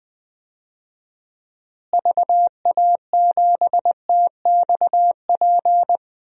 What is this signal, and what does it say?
Morse code, heard as a keyed tone, about 20 wpm, 700 Hz. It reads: VA7TXP
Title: DXCC-100 Call Signs - 6 Characters - Repeated 2x - 20wpm